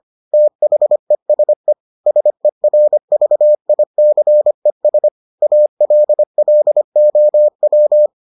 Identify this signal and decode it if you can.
Morse code, heard as a keyed tone, about 25 wpm, 600 Hz. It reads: THESE SERVICES ALLOW